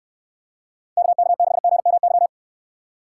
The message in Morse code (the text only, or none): KK6GMY